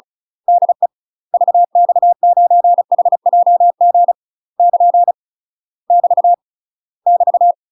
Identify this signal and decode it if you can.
Morse code, heard as a keyed tone, about 36 wpm, 700 Hz. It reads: DE VX9HJG <KN> <BT> <BT>
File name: UK MASTER QSOs TRAINING / 35WPM / UK MASTER QSOs 0026